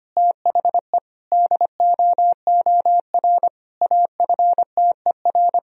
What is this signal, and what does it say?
Morse code, heard as a keyed tone, about 25 wpm, 700 Hz. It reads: THE DOOR AFTER
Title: Sentences from Top 1000 Words - Review Part 1 - 25wpm